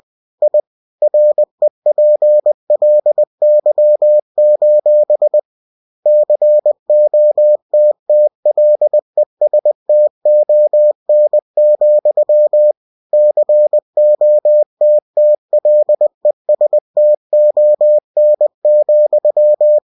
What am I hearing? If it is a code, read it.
Morse code, heard as a keyed tone, about 20 wpm, 600 Hz. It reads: I REPLY: COTTLESTON, COTTLESTON,
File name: cw-books / winnie-the-pooh / winnie-20wpm-page-0046